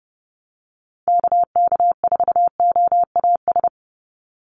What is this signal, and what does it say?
Morse code, heard as a keyed tone, about 28 wpm, 700 Hz. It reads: KK4OAS
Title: USA Call Signs 30wpm